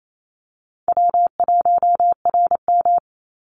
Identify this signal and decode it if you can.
Morse code, heard as a keyed tone, about 28 wpm, 700 Hz. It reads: W1RM
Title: Easy Call Signs Repeated 3x - 28wpm